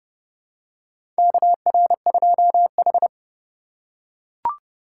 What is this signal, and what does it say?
Morse code, heard as a keyed tone, about 29 wpm, 700 Hz. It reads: KR2H
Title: USA Call Signs - 4x Character Spacing - 30wpm